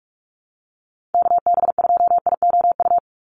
Morse code, heard as a keyed tone, about 44 wpm, 700 Hz. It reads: KB2IOU
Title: USA Call Signs 45wpm